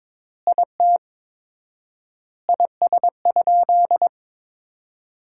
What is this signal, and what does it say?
Morse code, heard as a keyed tone, about 22 wpm, 700 Hz. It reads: IT IS?
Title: 4x Word-Spacing - Sentences from Top 100 Words - 22wpm